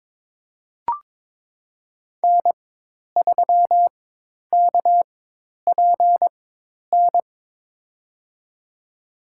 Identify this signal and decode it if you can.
Morse code, heard as a keyed tone, about 22 wpm, spaced as for 10 wpm, 700 Hz. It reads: N3KPN
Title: USA Call Signs - 4x Character Spacing - 22wpm